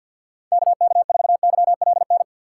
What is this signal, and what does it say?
Morse code, heard as a keyed tone, about 52 wpm, 700 Hz. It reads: KK4YLN
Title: USA Call Signs - 6 Characters - 3x Character Spacing - 50wpm